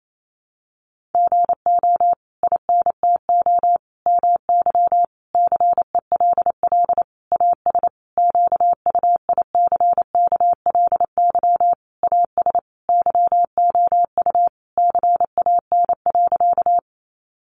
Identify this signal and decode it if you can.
Morse code, heard as a keyed tone, about 28 wpm, 700 Hz. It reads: GO INTO MY CELL AS QUICKLY AS YOU CAN.
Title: Sentences from Top 1100 Words 28wpm